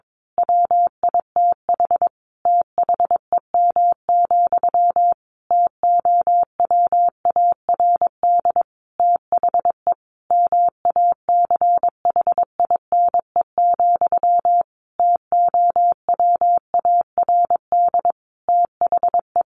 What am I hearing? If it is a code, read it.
Morse code, heard as a keyed tone, about 22 wpm, 700 Hz. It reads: WITH THEM, TOWARD THE MACHINE, TOWARD THE
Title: A Sound Of Thunder 22wpm